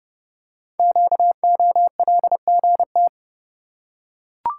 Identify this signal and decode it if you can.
Morse code, heard as a keyed tone, about 30 wpm, 690 Hz. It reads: QOLGT
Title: Code Groups - 5 Letters - 30wpm